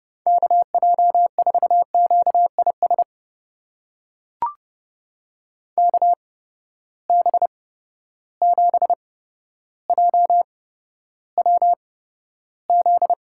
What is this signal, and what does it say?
Morse code, heard as a keyed tone, about 30 wpm, spaced as for 13 wpm, 700 Hz. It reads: KJ4QIS KB7JWZ
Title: USA Call Signs - 6 Characters - 8x Character Spacing - 30wpm